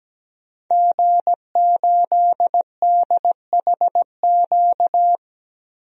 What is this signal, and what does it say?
Morse code, heard as a keyed tone, about 17 wpm, 700 Hz. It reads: G8DHQ
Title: DXCC-100 Call Signs - 5 Characters - 17wpm